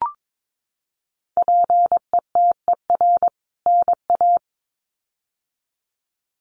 Morse code, heard as a keyed tone, about 22 wpm, 700 Hz. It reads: PETER NA
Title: CWT Exchange 22wpm